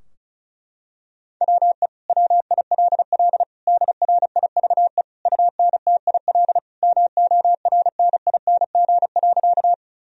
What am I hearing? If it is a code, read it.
Morse code, heard as a keyed tone, about 34 wpm, 700 Hz. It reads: WE WILL DRIVE UNTIL MORNING.
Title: Sentences from Top 1000 Words - Review Part 2 - 35wpm